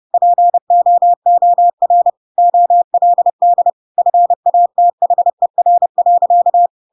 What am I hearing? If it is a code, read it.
Morse code, heard as a keyed tone, about 30 wpm, 700 Hz. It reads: POOR OLD FATHER.
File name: Pinocchio / Pinocchio ch3 30 30 ik1hge